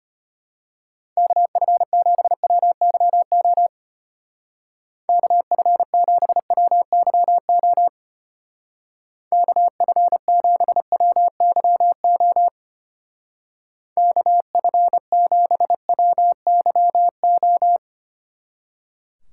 Morse code, heard as a keyed tone, about 31 wpm, 700 Hz. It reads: KF7WYO KF7WYO KF7WYO KF7WYO
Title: Speed Racing - USA 2x3 Call Signs 25wpm